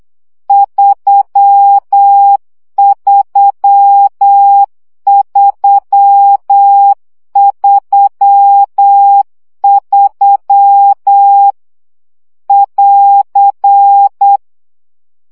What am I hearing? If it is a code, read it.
Morse code, heard as a keyed tone, about 8 wpm, 800 Hz. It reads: 33333 <AR>
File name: CW gyakorlas 3 7 baud